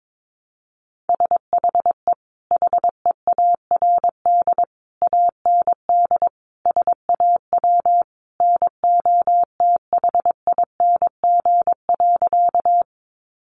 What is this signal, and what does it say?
Morse code, heard as a keyed tone, about 22 wpm, 700 Hz. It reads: SHE HEARD AND SAW NOTHING.
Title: Sentences from Top 500 Words 22wpm